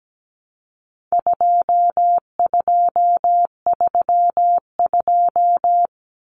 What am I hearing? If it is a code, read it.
Morse code, heard as a keyed tone, about 17 wpm, 700 Hz. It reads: 2232
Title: Common Cut Numbers - 4-Digit 17wpm